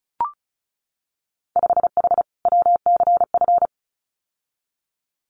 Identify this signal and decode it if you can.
Morse code, heard as a keyed tone, about 34 wpm, 700 Hz. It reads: 5H WCF
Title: Winter Field Day Exchanges Encore 35wpm